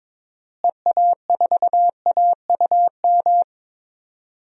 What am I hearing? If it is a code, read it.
Morse code, heard as a keyed tone, about 22 wpm, 700 Hz. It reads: EA4AUM
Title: DXCC-100 Call Signs - 6 Characters - Repeated 3x - 22wpm